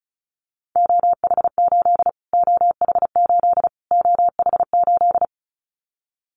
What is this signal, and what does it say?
Morse code, heard as a keyed tone, about 34 wpm, 700 Hz. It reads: OH8 OH8 OH8
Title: Callsign Prefixes for Top 25 DXCC - Repeated 3x - 35wpm